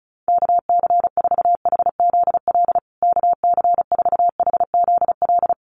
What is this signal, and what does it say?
Morse code, heard as a keyed tone, about 34 wpm, 700 Hz. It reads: KC4HZL KC4HZL
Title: USA Call Signs Repeated 3x 35wpm